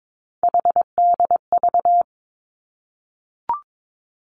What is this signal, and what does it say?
Morse code, heard as a keyed tone, about 22 wpm, 700 Hz. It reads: HDV
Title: Three Challenging Letter-Number Combinations 22wpm